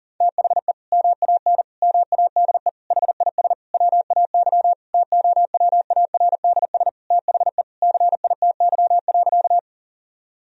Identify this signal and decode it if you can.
Morse code, heard as a keyed tone, about 40 wpm, 700 Hz. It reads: THE MAN MADE HIS WAY TOWARDS THE CITY.
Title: Sentences from Top 1000 Words - Review Part 2 - 40wpm